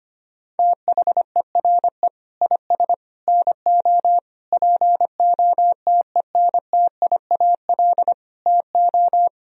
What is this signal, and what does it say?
Morse code, heard as a keyed tone, about 25 wpm, 700 Hz. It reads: THERE IS NO POTENTIAL TO
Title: Sentences from Top 1000 Words - Review Part 4 - 25wpm